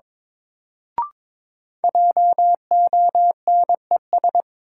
Morse code, heard as a keyed tone, about 22 wpm, 700 Hz. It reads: JONES
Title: Sets of 5 Words - New Word Drill - 22wpm